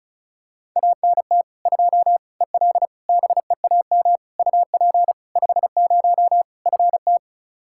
Morse code, heard as a keyed tone, about 36 wpm, 700 Hz. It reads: ANT 2 EL BEAM UP 50 FT
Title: Practice QSO - Exchange 2 - Info - 35wpm